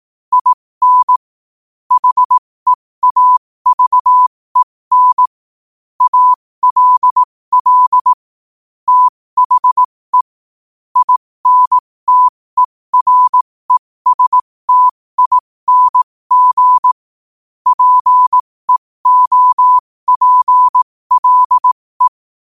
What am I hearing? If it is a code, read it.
Morse code, heard as a keyed tone, about 18 wpm, spaced as for 15 wpm, 1000 Hz. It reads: IN HEAVEN ALL THE INTERESTING PEOPLE
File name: qotdfiles / 15wpm